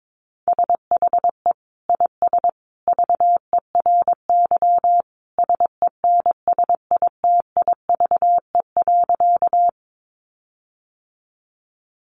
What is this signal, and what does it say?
Morse code, heard as a keyed tone, about 22 wpm, 700 Hz. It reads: SHE IS VERY SENSITIVE.
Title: Sentences from Top 1400 Words 22wpm